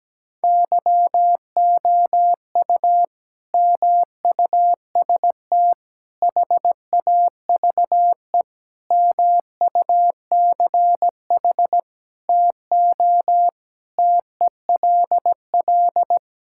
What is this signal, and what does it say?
Morse code, heard as a keyed tone, about 17 wpm, 700 Hz. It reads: YOU MUST HAVE MUCH TO TELL
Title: Sentences from Top 1600 Words 17wpm